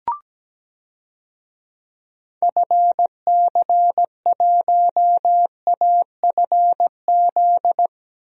Morse code, heard as a keyed tone, about 17 wpm, 700 Hz. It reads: FC1AFZ
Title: DXCC-100 Call Signs - 6 Characters - Repeated 2x - 17wpm